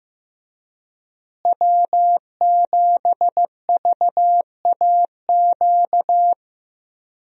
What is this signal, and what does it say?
Morse code, heard as a keyed tone, about 15 wpm, 700 Hz. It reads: W7VAQ
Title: USA Call Signs - 5 Characters - Repeated 2x - 15wpm